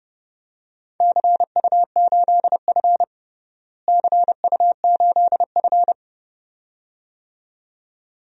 Morse code, heard as a keyed tone, about 30 wpm, 700 Hz. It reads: CU8F CU8F
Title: DXCC-100 Call Signs - 4 Characters - Repeated 2x - 30wpm